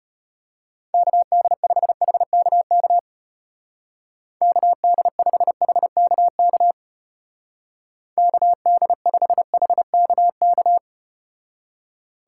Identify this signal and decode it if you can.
Morse code, heard as a keyed tone, about 33 wpm, 700 Hz. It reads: KD5HKK KD5HKK KD5HKK